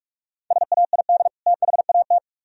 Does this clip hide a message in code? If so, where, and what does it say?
Yes: Morse code from 0.5 to 2.2 s, heard as a keyed tone, about 45 wpm, 700 Hz. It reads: SAID THAT